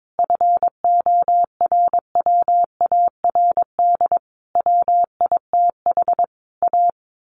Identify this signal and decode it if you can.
Morse code, heard as a keyed tone, about 22 wpm, 700 Hz. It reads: FORWARD WITH A
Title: A Sound Of Thunder 22wpm